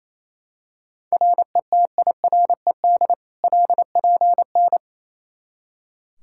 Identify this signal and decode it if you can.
Morse code, heard as a keyed tone, about 28 wpm, 700 Hz. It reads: RETIRED LPN
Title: Practice QSO - Exchange 3 - More Info - 28wpm